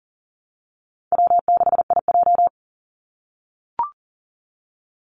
Morse code, heard as a keyed tone, about 40 wpm, 700 Hz. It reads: W6IJ
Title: Easy Call Signs - 40wpm